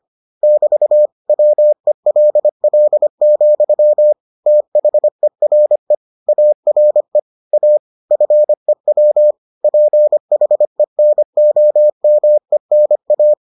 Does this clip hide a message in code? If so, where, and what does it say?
Yes: Morse code from 0.4 to 13.3 s, heard as a keyed tone, about 25 wpm, 600 Hz. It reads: <BT> WELL, THERE ARE A FEW PHENOMENA